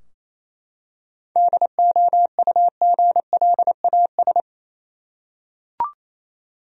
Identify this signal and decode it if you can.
Morse code, heard as a keyed tone, about 28 wpm, 700 Hz. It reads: DOUGLAS
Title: USA Common Names - 28wpm